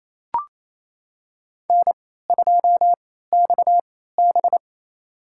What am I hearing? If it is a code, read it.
Morse code, heard as a keyed tone, about 28 wpm, spaced as for 16 wpm, 700 Hz. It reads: N2XB
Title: USA Call Signs - 3x Character Spacing - 28wpm